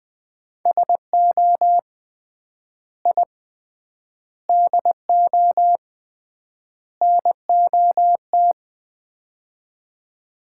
Morse code, heard as a keyed tone, about 21 wpm, 700 Hz. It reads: SO I DO NOT
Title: 3x Word-Spacing - Sets of 4 Words - 20wpm